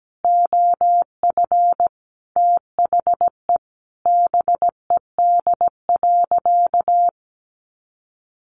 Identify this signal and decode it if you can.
Morse code, heard as a keyed tone, about 17 wpm, 700 Hz. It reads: OF THE BED.
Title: Sentences from Top 900 Words 17wpm